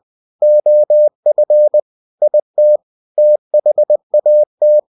Morse code, heard as a keyed tone, about 20 wpm, 600 Hz. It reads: OF IT THAT
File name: cw-books / winnie-the-pooh / winnie-20wpm-page-0092